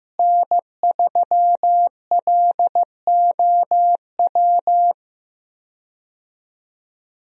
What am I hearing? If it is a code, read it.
Morse code, heard as a keyed tone, about 15 wpm, 700 Hz. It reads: N3LOW
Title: USA Call Signs - 5 Characters - Repeated 3x - 15wpm